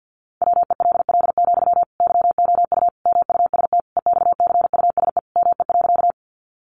Morse code, heard as a keyed tone, about 49 wpm, 700 Hz. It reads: WELL, YOU MUST EXCUSE ME.
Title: Sentences from Top 700 Words 50wpm